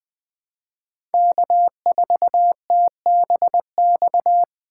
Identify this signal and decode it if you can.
Morse code, heard as a keyed tone, about 20 wpm, 700 Hz. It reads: K4TBX
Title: USA Call Signs - 5 Characters - 3x Character Spacing - 20wpm